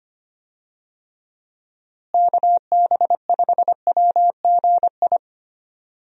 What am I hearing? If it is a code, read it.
Morse code, heard as a keyed tone, about 25 wpm, 700 Hz. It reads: KB5WGI
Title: USA Call Signs - 6 Characters - Repeated 3x - 25wpm